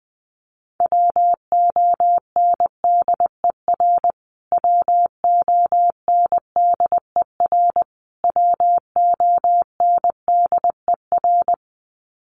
Morse code, heard as a keyed tone, about 20 wpm, 700 Hz. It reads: WONDER WONDER WONDER
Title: Top 500 Words Repeated 20wpm